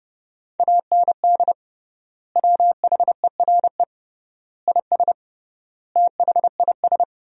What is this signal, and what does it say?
Morse code, heard as a keyed tone, about 29 wpm, 700 Hz. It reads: AND WHERE IS THIS
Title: 3x Word-Spacing - Sentences from Top 200 Words - 30wpm